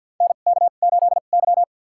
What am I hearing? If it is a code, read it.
Morse code, heard as a keyed tone, about 50 wpm, spaced as for 36 wpm, 700 Hz. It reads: NK8Y
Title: USA Call Signs - 4 Characters - 2x Character Spacing - 50wpm